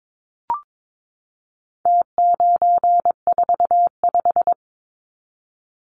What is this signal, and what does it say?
Morse code, heard as a keyed tone, about 22 wpm, 700 Hz. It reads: T945
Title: Numbers - Contest Style - 4-Digit 22wpm